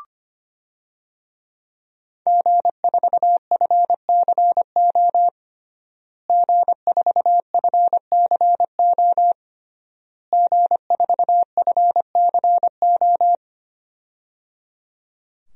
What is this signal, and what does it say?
Morse code, heard as a keyed tone, about 25 wpm, 700 Hz. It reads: G4FCO G4FCO G4FCO
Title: DXCC-100 Call Signs - 5 Characters - Repeated 3x - 25wpm